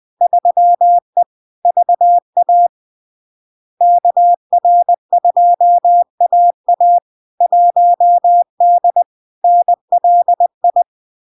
Morse code, heard as a keyed tone, about 20 wpm, 700 Hz. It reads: 3E VA KR2AA 1D NLI
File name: LARK-FD-2022-Log-in-CW-20wpm-700-Hz